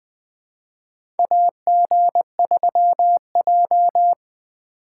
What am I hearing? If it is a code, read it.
Morse code, heard as a keyed tone, about 20 wpm, 700 Hz. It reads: AG3J
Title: USA Call Signs - 4 Characters - 2x Character Spacing - 20wpm